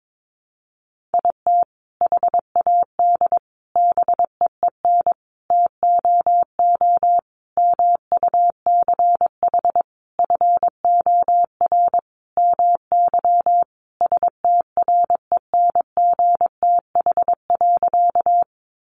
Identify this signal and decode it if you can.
Morse code, heard as a keyed tone, about 22 wpm, 700 Hz. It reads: IT HAD BEEN TOO MUCH FOR MY STRENGTH.
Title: Sentences from Top 1300 Words 22wpm